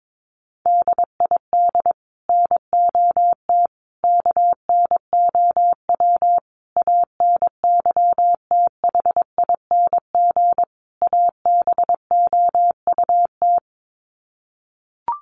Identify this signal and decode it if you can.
Morse code, heard as a keyed tone, about 22 wpm, 700 Hz. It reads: DID NOT KNOW ANYTHING ABOUT
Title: Sets of 5 Words Encore2 22wpm